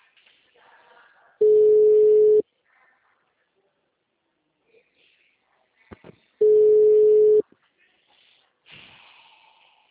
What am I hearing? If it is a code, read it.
Ringback tone (Europe, 425 Hz).
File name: out- -2003- - - .0